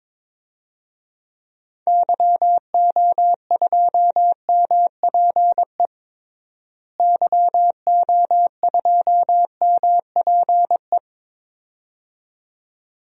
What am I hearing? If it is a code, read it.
Morse code, heard as a keyed tone, about 22 wpm, 700 Hz. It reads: YO2MPE YO2MPE
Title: DXCC-100 Call Signs - 6 Characters - Repeated 2x - 22wpm